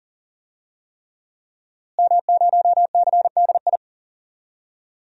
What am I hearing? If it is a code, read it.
Morse code, heard as a keyed tone, about 39 wpm, 700 Hz. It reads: M0CDI
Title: DXCC-100 Call Signs - 5 Characters - Repeated 3x - 40wpm